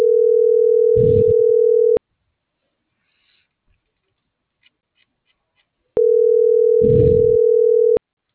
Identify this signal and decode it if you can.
Ringback tone (North America), 440 + 480 Hz.